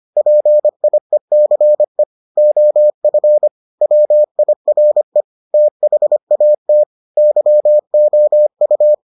Morse code, heard as a keyed tone, about 25 wpm, 600 Hz. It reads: PIECE OF WIRE THAT YOU